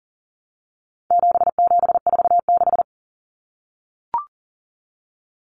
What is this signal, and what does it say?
Morse code, heard as a keyed tone, about 40 wpm, 700 Hz. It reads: 7746
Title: Numbers - 4-Digit 40wpm